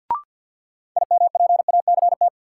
Morse code, heard as a keyed tone, about 47 wpm, 700 Hz. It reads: IMPACT
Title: Sentences from Top 1000 Words - Review Part 6 - New Word Drill - 50wpm